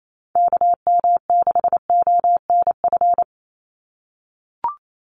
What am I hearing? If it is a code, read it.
Morse code, heard as a keyed tone, about 28 wpm, 700 Hz. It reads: KM6ONF
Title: USA Call Signs 28wpm